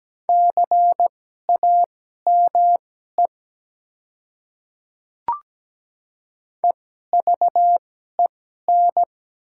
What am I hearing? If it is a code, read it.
Morse code, heard as a keyed tone, about 17 wpm, spaced as for 12 wpm, 700 Hz. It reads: CAME EVEN
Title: 2x Character Spacing - Top 200 Words - 17wpm